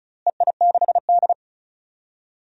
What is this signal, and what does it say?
Morse code, heard as a keyed tone, about 36 wpm, 700 Hz. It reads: EI6D
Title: DXCC-100 Call Signs - 4 Characters - Repeated 2x - 35wpm